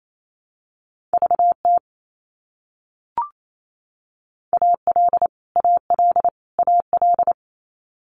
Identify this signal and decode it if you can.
Morse code, heard as a keyed tone, about 28 wpm, 700 Hz. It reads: VT AL AL AL
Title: US States Repeated 3x 28wpm